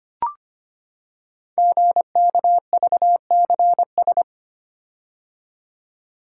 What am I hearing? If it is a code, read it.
Morse code, heard as a keyed tone, about 25 wpm, 700 Hz. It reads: GKVCS
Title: Code Groups - 5 Letters - 25wpm